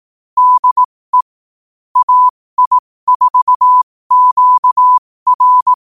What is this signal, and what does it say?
Morse code, heard as a keyed tone, about 18 wpm, spaced as for 15 wpm, 1000 Hz. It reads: DE AI4QR
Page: 15 WPM morse code quotes